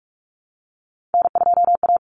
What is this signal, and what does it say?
Morse code, heard as a keyed tone, about 44 wpm, 700 Hz. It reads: N2A
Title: Common Cut Numbers - 3-Digit 45wpm